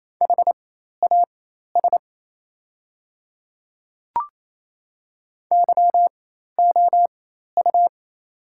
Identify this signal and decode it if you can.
Morse code, heard as a keyed tone, about 28 wpm, spaced as for 13 wpm, 700 Hz. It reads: HAS YOU
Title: 4x Character Spacing - Top 100 Words - 28wpm